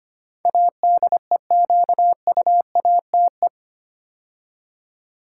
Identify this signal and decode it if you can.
Morse code, heard as a keyed tone, about 25 wpm, 700 Hz. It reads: ADEQUATE
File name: 8 Letter Words - First 3 Letters Given - 25wpm